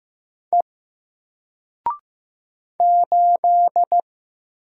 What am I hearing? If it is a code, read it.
Morse code, heard as a keyed tone, about 15 wpm, 700 Hz. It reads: E 8
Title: Single Letter-Number 15wpm